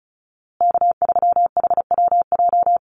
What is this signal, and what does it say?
Morse code, heard as a keyed tone, about 34 wpm, 700 Hz. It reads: K3HWJ
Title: USA Call Signs Repeated 3x 35wpm